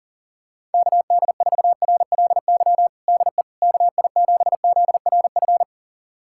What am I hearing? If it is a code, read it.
Morse code, heard as a keyed tone, about 37 wpm, 700 Hz. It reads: KD4RLY DE KI7ZRF
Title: USA Callsign DE Callsign 40wpm